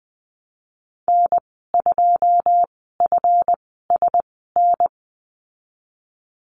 Morse code, heard as a keyed tone, about 20 wpm, spaced as for 15 wpm, 700 Hz. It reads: N2FSN
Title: 147 Lesson 24g - Letter y - Call Signs 20wpm